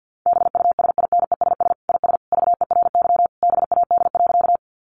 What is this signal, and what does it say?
Morse code, heard as a keyed tone, about 50 wpm, 700 Hz. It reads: BUSINESS IS VERY BAD.
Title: Sentences from Top 700 Words 50wpm